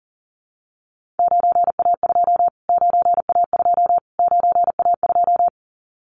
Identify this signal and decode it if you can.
Morse code, heard as a keyed tone, about 37 wpm, 700 Hz. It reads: 9A2 9A2 9A2
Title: Callsign Prefixes for Top 25 DXCC - Repeated 3x - 40wpm